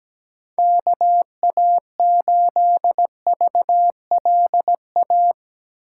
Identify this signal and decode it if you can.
Morse code, heard as a keyed tone, about 17 wpm, 700 Hz. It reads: KA8VLA
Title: USA Call Signs - 2x Character Spacing - 17wpm